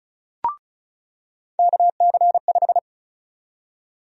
Morse code, heard as a keyed tone, about 34 wpm, 700 Hz. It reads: KC5
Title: USA Call Sign Prefixes - 35wpm